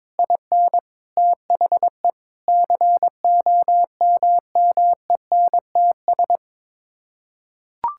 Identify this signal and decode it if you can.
Morse code, heard as a keyed tone, about 22 wpm, 700 Hz. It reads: IN THE COMMENTS
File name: practice / 22wpm / Sets of 8 Words - 22wpm